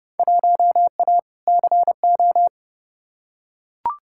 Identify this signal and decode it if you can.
Morse code, heard as a keyed tone, about 30 wpm, 700 Hz. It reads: 1A CO
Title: Field Day Exchanges - 30wpm